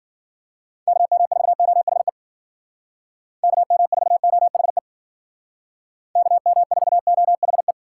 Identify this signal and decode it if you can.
Morse code, heard as a keyed tone, about 53 wpm, 700 Hz. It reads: KM4OHE KM4OHE KM4OHE